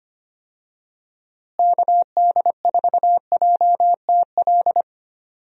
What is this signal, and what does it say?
Morse code, heard as a keyed tone, about 25 wpm, 700 Hz. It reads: KD4JTL
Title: USA Call Signs - 6 Characters - Repeated 2x - 25wpm